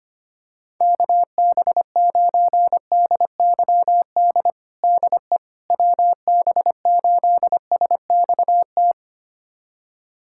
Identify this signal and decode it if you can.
Morse code, heard as a keyed tone, about 25 wpm, 700 Hz. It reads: KB9DYD DE WB8SXT
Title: USA Callsign DE Callsign - 6 Characters - 25wpm